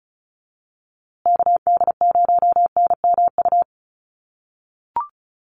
Morse code, heard as a keyed tone, about 34 wpm, 700 Hz. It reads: KD0NMU
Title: USA Call Signs Repeated 2x 35wpm